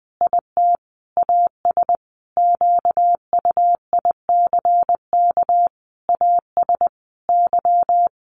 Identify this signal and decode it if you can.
Morse code, heard as a keyed tone, about 20 wpm, 700 Hz. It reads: IT AS QUICK AS Y
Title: Sentences from Top 1400 Words 20wpm